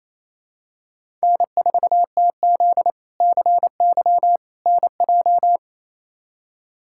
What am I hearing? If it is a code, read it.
Morse code, heard as a keyed tone, about 28 wpm, 700 Hz. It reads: N4TZ CY NJ
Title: K1USN SST QSO Practice - 28wpm